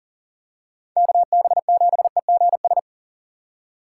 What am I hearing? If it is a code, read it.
Morse code, heard as a keyed tone, about 40 wpm, 700 Hz. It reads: KB7EGS
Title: USA Call Signs - 2x Character Spacing - 40wpm